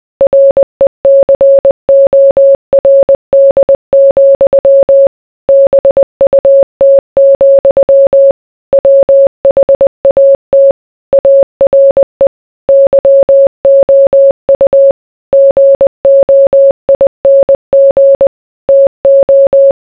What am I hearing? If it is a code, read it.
Morse code, heard as a keyed tone, about 20 wpm, 550 Hz. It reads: RECORD, BUT, WHAT ARE YOU GOING TO